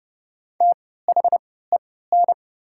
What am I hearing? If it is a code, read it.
Morse code, heard as a keyed tone, about 30 wpm, spaced as for 17 wpm, 700 Hz. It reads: THEN
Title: 3x Character Spacing - Top 200 Words - 30wpm